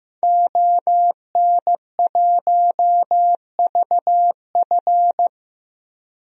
Morse code, heard as a keyed tone, about 15 wpm, 700 Hz. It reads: ON1VF
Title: DXCC-100 Call Signs - 5 Characters - Repeated 3x - 15wpm